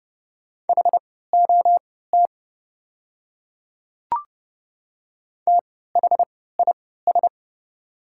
Morse code, heard as a keyed tone, about 31 wpm, spaced as for 17 wpm, 690 Hz. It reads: HOT THIS
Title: 3x Character Spacing - Top 100 Words - 30wpm